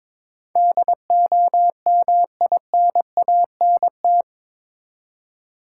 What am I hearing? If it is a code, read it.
Morse code, heard as a keyed tone, about 22 wpm, 700 Hz. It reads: DOMINANT
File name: practice / 22wpm / Eight Letter Words - 22wpm